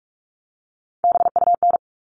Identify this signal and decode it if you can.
Morse code, heard as a keyed tone, about 44 wpm, 700 Hz. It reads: BUN
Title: All Cut Numbers - 3-Digit 45wpm